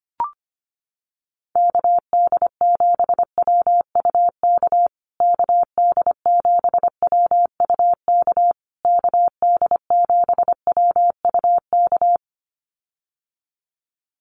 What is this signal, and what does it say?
Morse code, heard as a keyed tone, about 25 wpm, 700 Hz. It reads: KD7WUK KD7WUK KD7WUK
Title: USA Call Signs Repeated 3x 25wpm